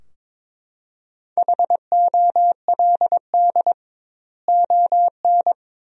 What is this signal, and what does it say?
Morse code, heard as a keyed tone, about 22 wpm, 700 Hz. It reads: HOLD ON
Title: 2x Word-Spacing - Sentences from Top 500 Words - 22wpm